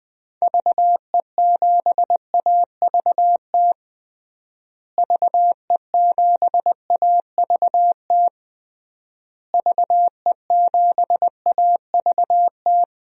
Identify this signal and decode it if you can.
Morse code, heard as a keyed tone, about 20 wpm, 700 Hz. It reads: VE7AVT VE7AVT VE7AVT
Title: DXCC-100 Call Signs - 6 Characters - Repeated 3x - 20wpm